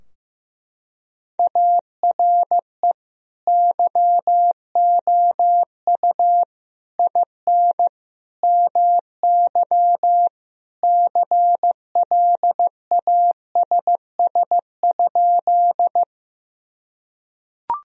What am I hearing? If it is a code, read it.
Morse code, heard as a keyed tone, about 15 wpm, 700 Hz. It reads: ARE YOU IN MY CLASS?
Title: Sentences from Top 1000 Words - Review Part 2 - 15wpm